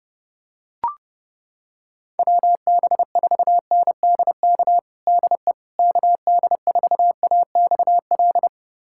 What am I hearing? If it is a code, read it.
Morse code, heard as a keyed tone, about 32 wpm, 700 Hz. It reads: WB4NDK DE KD4AXL
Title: USA Callsign DE Callsign - 6 Characters - 30wpm